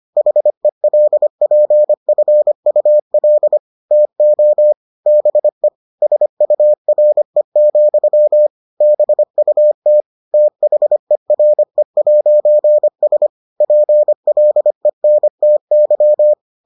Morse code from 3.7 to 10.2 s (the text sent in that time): TO BE SURE, BUT